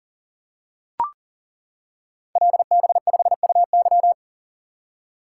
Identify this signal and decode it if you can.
Morse code, heard as a keyed tone, about 40 wpm, 700 Hz. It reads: LB5UY
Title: Code Groups - 5 Letter:Numbers - 40wpm